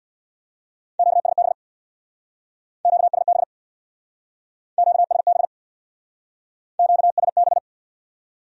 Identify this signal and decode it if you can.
Morse code, heard as a keyed tone, about 62 wpm, 700 Hz. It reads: QSB QSB QSB QSB
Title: Speed-Racing Q-Codes - 50wpm